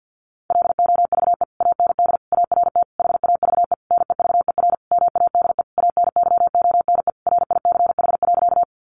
Morse code, heard as a keyed tone, about 50 wpm, 700 Hz. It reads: LOVE AND ART HAVE NEVER MADE ANYONE RICH.
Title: Sentences from Top 1100 Words 50wpm